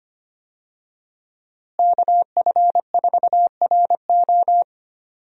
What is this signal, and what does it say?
Morse code, heard as a keyed tone, about 25 wpm, 700 Hz. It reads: KF4RO
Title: USA Call Signs - 5 Characters - Repeated 2x - 25wpm